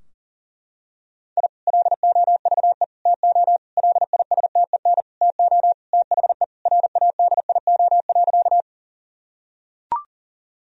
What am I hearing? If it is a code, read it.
Morse code, heard as a keyed tone, about 40 wpm, 700 Hz. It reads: I LOVE TO LISTEN TO THE RADIO.
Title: Sentences from Top 1000 Words - Review Part 3 - 40wpm